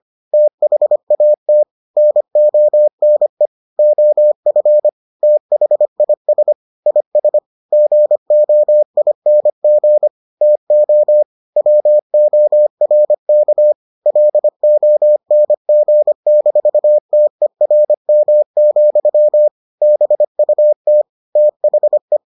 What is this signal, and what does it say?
Morse code, heard as a keyed tone, about 25 wpm, 600 Hz. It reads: THAT NONE OF THIS IS GOING TO WORK LONG-TERM, BUT THE